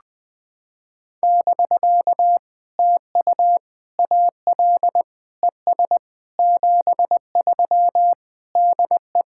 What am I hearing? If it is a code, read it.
Morse code, heard as a keyed tone, about 20 wpm, 700 Hz. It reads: <BK> TU AL ES 73 DE
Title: POTA QSO Practice - 20wpm